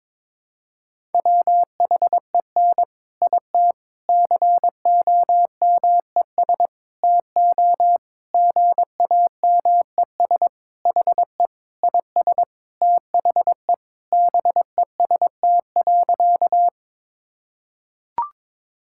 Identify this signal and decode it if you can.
Morse code, heard as a keyed tone, about 22 wpm, 700 Hz. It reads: WHEN IT COMES TO GAMES HE IS THE BEST.
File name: practice / 22wpm / Sentences with 3-Word Phrases - 22wpm